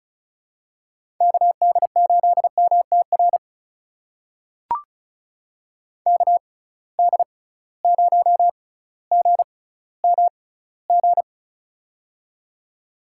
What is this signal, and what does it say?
Morse code, heard as a keyed tone, about 34 wpm, spaced as for 18 wpm, 700 Hz. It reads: KD8MTR KD0GMG
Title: USA Call Signs - 6 Characters - 6x Character Spacing - 35wpm